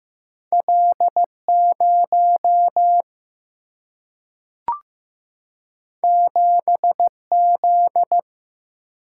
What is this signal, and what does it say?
Morse code, heard as a keyed tone, about 15 wpm, 700 Hz. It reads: L0 7Z
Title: Single-Letter Single-Number 15wpm